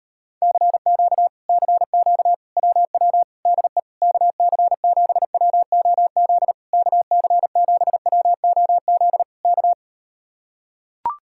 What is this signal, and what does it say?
Morse code, heard as a keyed tone, about 38 wpm, 700 Hz. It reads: CQ CQ WW DE KC7WOZ KC7WOZ K